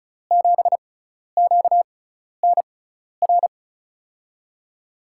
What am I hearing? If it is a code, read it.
Morse code, heard as a keyed tone, about 36 wpm, spaced as for 12 wpm, 700 Hz. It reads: 7QNR
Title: USA Call Signs - 6 Characters - 6x Character Spacing - 35wpm